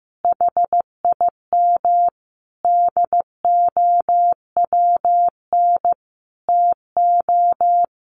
Morse code, heard as a keyed tone, about 14 wpm, 700 Hz. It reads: HIM DOWN TO
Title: Sentences from Top 1500 Words 15wpm